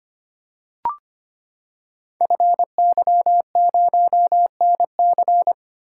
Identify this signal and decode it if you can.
Morse code, heard as a keyed tone, about 25 wpm, 700 Hz. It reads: FY0NC
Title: Code Groups - 5 Letter:Numbers - 25wpm